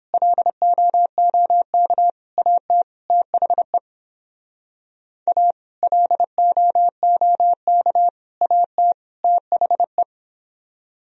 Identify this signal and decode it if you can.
Morse code, heard as a keyed tone, about 31 wpm, 700 Hz. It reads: LOOK AT THE A LOOK AT THE